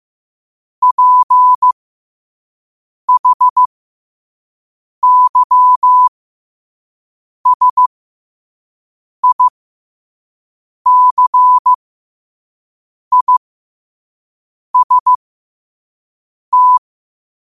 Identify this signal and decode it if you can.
Morse code, heard as a keyed tone, about 15 wpm, spaced as for 5 wpm, 1000 Hz. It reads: PHYSICIST
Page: New quotes every day in morse code at 5 Words per minute.